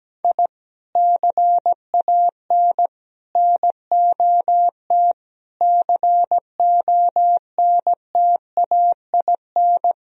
Morse code, heard as a keyed tone, about 17 wpm, 700 Hz. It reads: I CAN NOT CONTAIN
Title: Sentences from Top 1000 Words - Review Part 3 - 17wpm